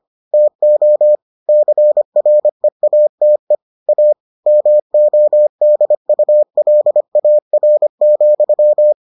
Morse code, heard as a keyed tone, about 25 wpm, 600 Hz. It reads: TO CREATE A MODULAR,